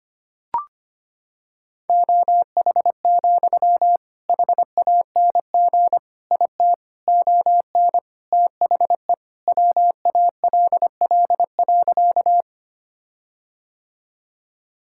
Morse code, heard as a keyed tone, about 25 wpm, 700 Hz. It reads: OH, HANG IT ON THE WALL.
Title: Sentences from Top 1600 Words 25wpm